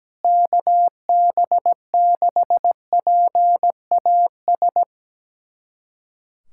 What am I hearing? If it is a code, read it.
Morse code, heard as a keyed tone, about 17 wpm, 700 Hz. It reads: KB6PAS
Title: USA Callsign DE Callsign 17wpm